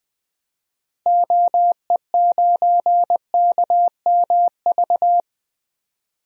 Morse code, heard as a keyed tone, about 20 wpm, 700 Hz. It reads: OE9KMV
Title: DXCC-100 Call Signs - 6 Characters - Repeated 3x - 20wpm